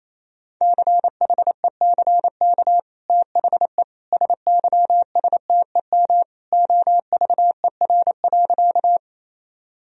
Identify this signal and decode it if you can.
Morse code, heard as a keyed tone, about 28 wpm, 700 Hz. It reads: CHECK THE SYSTEM OVER.
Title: Sentences from Top 1000 Words - Review Part 3 - 28wpm